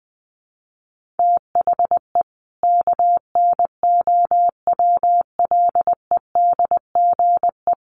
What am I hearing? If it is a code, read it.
Morse code, heard as a keyed tone, about 20 wpm, 700 Hz. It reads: THE KNOWLEDGE
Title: Sentences from Top 1400 Words 20wpm